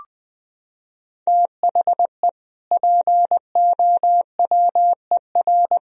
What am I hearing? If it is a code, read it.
Morse code, heard as a keyed tone, about 20 wpm, 700 Hz. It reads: THE POWER
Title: Sentences from Top 1000 Words - Review Part 4 - 20wpm